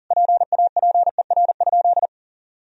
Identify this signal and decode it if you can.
Morse code, heard as a keyed tone, about 37 wpm, 700 Hz. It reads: PAPER?
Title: Sentences from Top 1000 Words - Review Part 2 - 40wpm